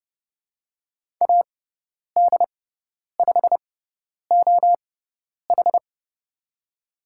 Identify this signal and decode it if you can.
Morse code, heard as a keyed tone, about 31 wpm, spaced as for 10 wpm, 700 Hz. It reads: AD5OH
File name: Lesson 17g - Letter u - Call Signs 30wpm